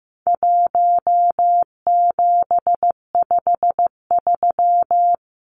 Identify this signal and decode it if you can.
Morse code, heard as a keyed tone, about 15 wpm, 700 Hz. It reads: 1753
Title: Numbers - 4-Digit 15wpm